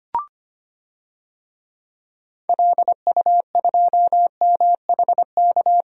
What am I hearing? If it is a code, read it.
Morse code, heard as a keyed tone, about 25 wpm, 700 Hz. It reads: LU2MHK
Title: DXCC-100 Call Signs - 6 Characters - Repeated 3x - 25wpm